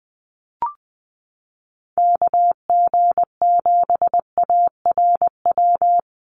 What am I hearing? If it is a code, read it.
Morse code, heard as a keyed tone, about 20 wpm, 700 Hz. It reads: KG7ARW
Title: USA Call Signs Repeated 3x 20wpm